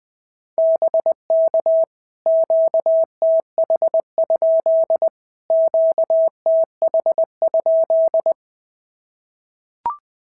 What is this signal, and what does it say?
Morse code, heard as a keyed tone, about 19 wpm, 650 Hz. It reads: BK QTH? QTH?
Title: POTA QSO Practice - 20wpm